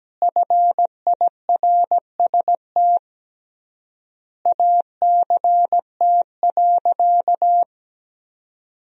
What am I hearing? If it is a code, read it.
Morse code, heard as a keyed tone, about 17 wpm, 700 Hz. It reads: FIRST ACT.
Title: 3x Word-Spacing - Sentences from Top 200 Words - 17wpm